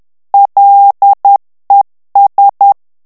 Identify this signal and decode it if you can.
Morse code, heard as a keyed tone, about 10 wpm, 800 Hz. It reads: LES
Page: Français - texte en clair 1 - vitesse : 10 mots minute